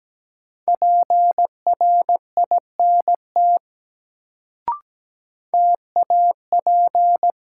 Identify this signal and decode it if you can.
Morse code, heard as a keyed tone, about 17 wpm, 700 Hz. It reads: PRINT TAP
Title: Sentences from Top 1400 Words - New Word Drill Alt - 17wpm